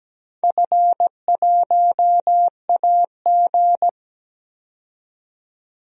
Morse code, heard as a keyed tone, about 17 wpm, 700 Hz. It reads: F1AG
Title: DXCC-100 Call Signs - 4 Characters - 17wpm